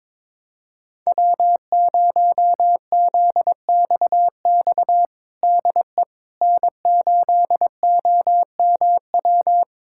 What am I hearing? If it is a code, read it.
Morse code, heard as a keyed tone, about 22 wpm, 700 Hz. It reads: W0ZXX DE N8OMW
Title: USA Callsign DE Callsign - 5 Characters - 22wpm